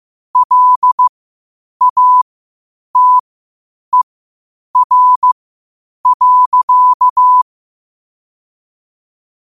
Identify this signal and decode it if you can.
Morse code, heard as a keyed tone, about 15 wpm, spaced as for 8 wpm, 1000 Hz. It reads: LATER.
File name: qotdfiles / 8wpm